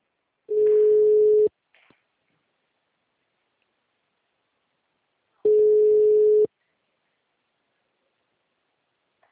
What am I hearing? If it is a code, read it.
Ringback tone (Europe, 425 Hz).